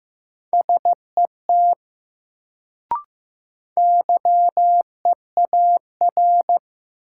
Morse code, heard as a keyed tone, about 15 wpm, 700 Hz. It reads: SET YEAR
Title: Sentences from Top 200 Words - New Word Drill - 15wpm